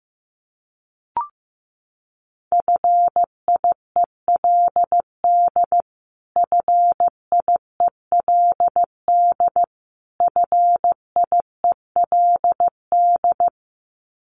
Morse code, heard as a keyed tone, about 15 wpm, 700 Hz. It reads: FIELD FIELD FIELD
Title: Top 500 Words Repeated 15wpm